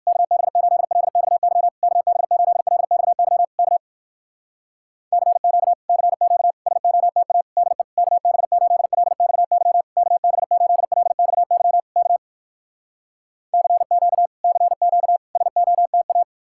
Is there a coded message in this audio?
Yes: Morse code from 0.1 to 16.2 s, heard as a keyed tone, about 59 wpm, 700 Hz. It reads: KB8LXY KB8LXY K CQ CQ SOTA DE KB8LXY KB8LXY K CQ CQ SOTA